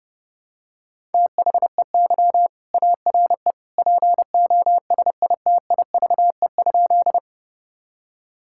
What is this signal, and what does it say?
Morse code, heard as a keyed tone, about 30 wpm, 700 Hz. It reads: THEY ARE POSITIVE?
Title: Sentences from Top 1000 Words - Review Part 5 - 30wpm